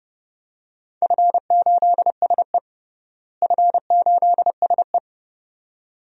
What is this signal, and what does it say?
Morse code, heard as a keyed tone, about 32 wpm, 700 Hz. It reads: F8SE F8SE
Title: DXCC-100 Call Signs - 4 Characters - Repeated 2x - 30wpm